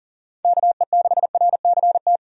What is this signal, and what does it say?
Morse code, heard as a keyed tone, about 40 wpm, 700 Hz. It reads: KE6RCT
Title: USA Call Signs - 6 Characters - 40wpm